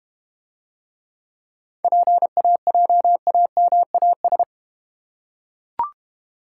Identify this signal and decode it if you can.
Morse code, heard as a keyed tone, about 31 wpm, 700 Hz. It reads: PAJAMAS
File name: Easy Words - Speed-Racing with Hints - 25wpm